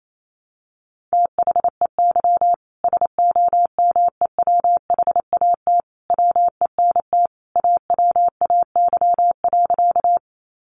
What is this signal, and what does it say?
Morse code, heard as a keyed tone, about 28 wpm, 700 Hz. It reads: THEY SOMEWHAT WENT AWAY.
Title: Sentences from Top 1300 Words 28wpm